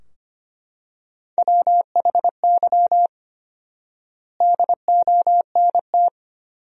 Morse code, heard as a keyed tone, about 25 wpm, 700 Hz. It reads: WHY DONT
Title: 4x Word-Spacing - Sentences from Top 300 Words - 25wpm